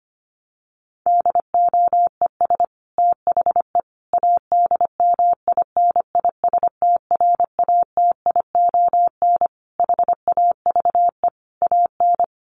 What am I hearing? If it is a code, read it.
Morse code, heard as a keyed tone, about 25 wpm, 700 Hz. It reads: DOES THE ADMINISTRATION HAVE AN
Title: Sentences from Top 1300 Words 25wpm